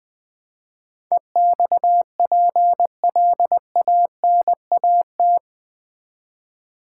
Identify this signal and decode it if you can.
Morse code, heard as a keyed tone, about 20 wpm, 700 Hz. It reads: EXPLANAT
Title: Build A Word - 4 Letters At A Time - 11-Letter Words - 20wpm